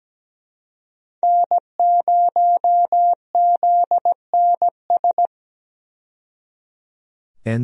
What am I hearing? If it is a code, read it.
Morse code, heard as a keyed tone, about 17 wpm, 700 Hz. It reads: N0ZNS
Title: USA Call Signs - 5 Characters - 17wpm